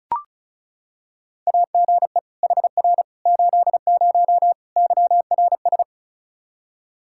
Morse code, heard as a keyed tone, about 36 wpm, 700 Hz. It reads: AGE HR 80 YRS
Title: K1USN SST QSO Practice 35wpm